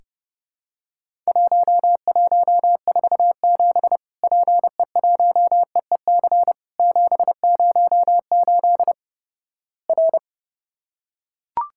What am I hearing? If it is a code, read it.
Morse code, heard as a keyed tone, about 32 wpm, 700 Hz. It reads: 1147 PE1EEC 708 R
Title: Worked All Europe DX Contest - QTC Exchange - 30wpm